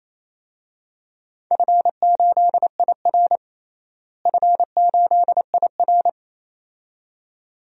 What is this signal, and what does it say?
Morse code, heard as a keyed tone, about 28 wpm, 700 Hz. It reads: F8IR F8IR
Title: DXCC-100 Call Signs - 4 Characters - Repeated 2x - 28wpm